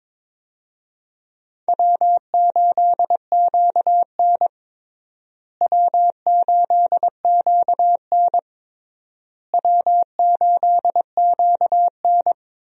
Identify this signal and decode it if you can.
Morse code, heard as a keyed tone, about 22 wpm, 700 Hz. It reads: W8QN W8QN W8QN
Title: USA Call Signs - 4 Characters - Repeated 3x - 22wpm